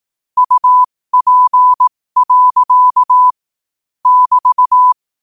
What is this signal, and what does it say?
Morse code, heard as a keyed tone, about 18 wpm, spaced as for 15 wpm, 1000 Hz. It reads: UP. <BT>
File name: qotdfiles / 15wpm